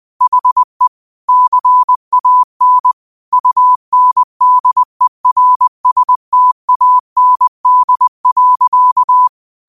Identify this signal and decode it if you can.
Morse code, heard as a keyed tone, about 20 wpm, 1000 Hz. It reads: HE CAN UNDERSTAND.
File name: qotdfiles / 20wpm